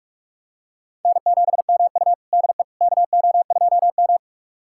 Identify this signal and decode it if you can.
Morse code, heard as a keyed tone, about 46 wpm, 700 Hz. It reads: N7MU DE KO2M
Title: USA Callsign DE Callsign - 4 Characters - 45wpm